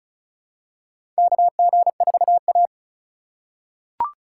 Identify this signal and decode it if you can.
Morse code, heard as a keyed tone, about 36 wpm, 700 Hz. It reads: KG4A
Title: USA Call Signs - 4 Characters - 3x Character Spacing - 35wpm